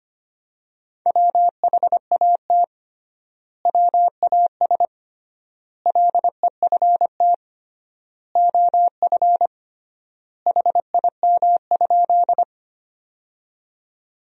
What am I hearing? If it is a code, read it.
Morse code, heard as a keyed tone, about 25 wpm, 700 Hz. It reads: WHAT WAS LEFT OF HIM?
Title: 3x Word-Spacing - Sentences from Top 300 Words - 25wpm